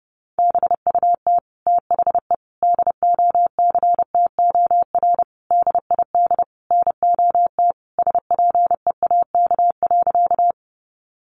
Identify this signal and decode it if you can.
Morse code, heard as a keyed tone, about 30 wpm, 700 Hz. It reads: BUT THE DOCTOR DID NOT SPEAK.
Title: Sentences from Top 1500 Words 30wpm